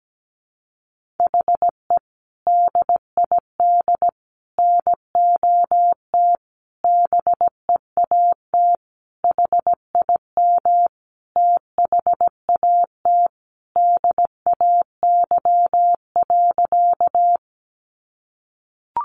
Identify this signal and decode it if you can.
Morse code, heard as a keyed tone, about 17 wpm, 700 Hz. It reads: HE DID NOT BEAT HIM THAT DAY.
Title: Sentences from Top 1300 Words 17wpm